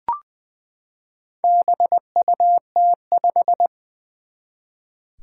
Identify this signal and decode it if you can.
Morse code, heard as a keyed tone, about 20 wpm, 700 Hz. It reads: BUT5
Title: Four Challenging Letter-Number Combinations 20wpm